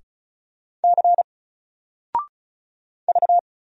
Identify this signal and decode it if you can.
Morse code, heard as a keyed tone, about 36 wpm, 700 Hz. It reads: C V
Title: Single Character (Letter:Number:Punctuation:Prosign) 35wpm